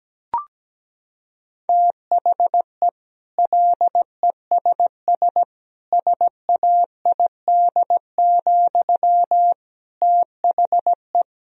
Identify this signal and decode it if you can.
Morse code, heard as a keyed tone, about 17 wpm, 700 Hz. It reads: THE LESS SAID, THE
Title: Sentences from Top 1000 Words - Review Part 2 - 17wpm